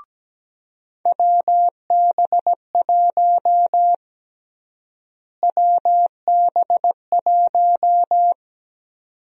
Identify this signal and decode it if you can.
Morse code, heard as a keyed tone, about 17 wpm, 700 Hz. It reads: WB1 WB1
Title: USA Call Sign Prefixes - Repeated 2x - 17wpm